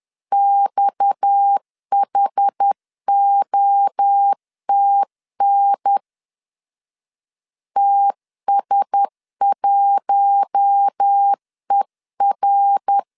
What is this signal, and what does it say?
Morse code, heard as a keyed tone, about 10 wpm, 800 Hz. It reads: XHOTN TS1ER
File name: CW / DET / 1 6 1500C S C 10wpm 4252